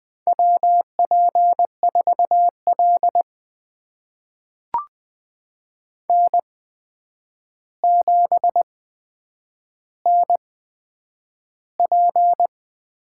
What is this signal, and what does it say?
Morse code, heard as a keyed tone, about 20 wpm, spaced as for 9 wpm, 700 Hz. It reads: WP4L N7NP
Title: USA Call Signs - 4 Characters - 8x Character Spacing - 20wpm